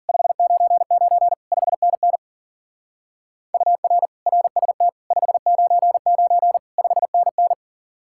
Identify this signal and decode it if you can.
Morse code, heard as a keyed tone, about 50 wpm, 700 Hz. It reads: 599 5NN UR RST 599 5NN
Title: Speed-Racing Common QSO Phrases - 40wpm